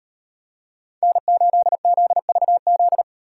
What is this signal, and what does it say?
Morse code, heard as a keyed tone, about 38 wpm, 700 Hz. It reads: N8ZVZ